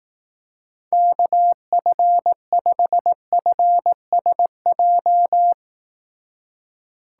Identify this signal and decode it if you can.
Morse code, heard as a keyed tone, about 18 wpm, 700 Hz. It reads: KF5FSJ
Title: Speed Racing - USA 2x3 Call Signs 15wpm